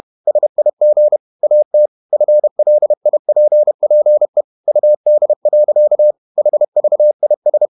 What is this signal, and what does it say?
Morse code, heard as a keyed tone, about 31 wpm, 600 Hz. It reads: SIG AT FLIPPE UD. HVIS
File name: Ferie QRQ 31wpm